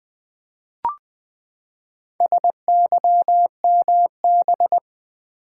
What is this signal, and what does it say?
Morse code, heard as a keyed tone, about 20 wpm, 700 Hz. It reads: SYMB
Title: Build A Word - 4 Letters At A Time - 8-Letter Words - 20wpm